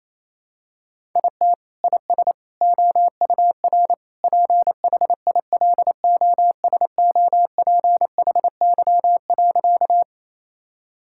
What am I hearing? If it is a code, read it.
Morse code, heard as a keyed tone, about 28 wpm, 700 Hz. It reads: IT IS OUR PHILOSOPHY.
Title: Sentences from Top 1000 Words - Review Part 6 - 28wpm